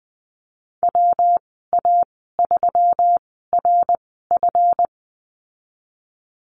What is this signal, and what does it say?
Morse code, heard as a keyed tone, about 19 wpm, spaced as for 14 wpm, 700 Hz. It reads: WA3RF
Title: 154 Lesson 25g - Letter p - Call Signs 20wpm